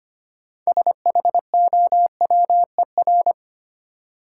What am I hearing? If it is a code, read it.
Morse code, heard as a keyed tone, about 25 wpm, 700 Hz. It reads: SHOWER
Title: Sentences from Top 1600 Words - New Word Drill Alt - 25wpm